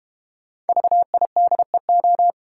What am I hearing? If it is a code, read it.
Morse code, heard as a keyed tone, about 32 wpm, 700 Hz. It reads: VIDEO